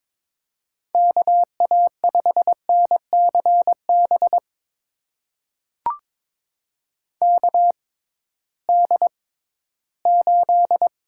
Morse code, heard as a keyed tone, about 21 wpm, spaced as for 14 wpm, 700 Hz. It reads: KA5NCB KD8
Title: USA Call Signs - 6 Characters - 6x Character Spacing - 22wpm